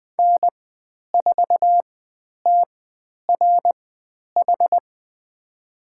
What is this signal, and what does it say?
Morse code, heard as a keyed tone, about 20 wpm, spaced as for 10 wpm, 700 Hz. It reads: N4TRH
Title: 095 Lesson 16g - Letter c - Call Signs 20wpm